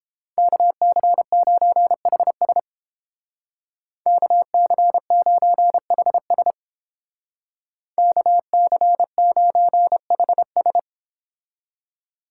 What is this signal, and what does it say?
Morse code, heard as a keyed tone, about 29 wpm, 700 Hz. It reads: KC9HS KC9HS KC9HS